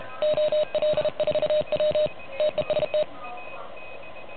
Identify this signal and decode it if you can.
Morse code, heard as a keyed tone, about 32 wpm, 600 Hz. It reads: OL4W TEST
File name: truSDX / MWC29.8